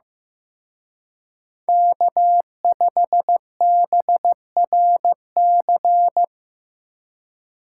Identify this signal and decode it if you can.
Morse code, heard as a keyed tone, about 18 wpm, 700 Hz. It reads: K5BRC
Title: Speed Racing - USA 1x3 Call Signs 15wpm